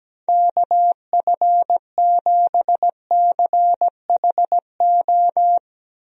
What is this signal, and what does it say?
Morse code, heard as a keyed tone, about 17 wpm, 700 Hz. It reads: KF7CHO
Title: USA Call Signs - 6 Characters - 17wpm